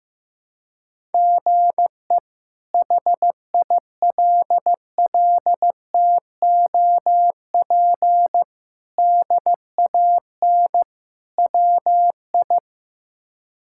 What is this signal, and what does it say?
Morse code, heard as a keyed tone, about 15 wpm, 700 Hz. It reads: GE HILLTOP DAN WI
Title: K1USN SST QSO Practice - 15wpm